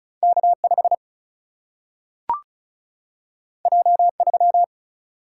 Morse code, heard as a keyed tone, about 36 wpm, 700 Hz. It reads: K5 J3
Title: Single-Letter Single-Number 35wpm